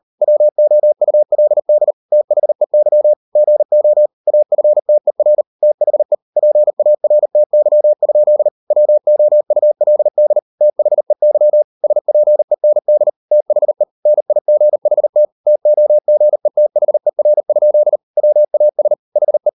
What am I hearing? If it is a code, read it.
Morse code, heard as a keyed tone, about 40 wpm, 600 Hz. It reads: WOULD THEY GO AFTER THE PARTY? WOULD THEY SPEND THE NIGHT TOGETHER? WAS HE